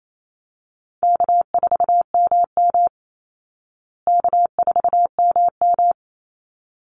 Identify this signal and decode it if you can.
Morse code, heard as a keyed tone, about 28 wpm, 700 Hz. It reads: K4MM K4MM
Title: Call Signs Repeated 3x - 28wpm